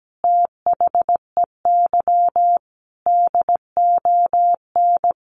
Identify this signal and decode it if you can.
Morse code, heard as a keyed tone, about 17 wpm, 700 Hz. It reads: THEY DON
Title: Easy Phrases 17wpm